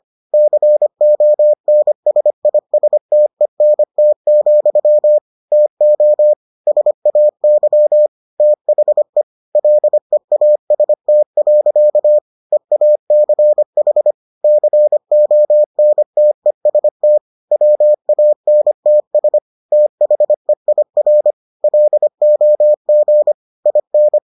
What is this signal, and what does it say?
Morse code, heard as a keyed tone, about 25 wpm, 600 Hz. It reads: CONSISTENT, TO SAY THE LEAST. EACH CONTEST WANTS THEIR LOG IN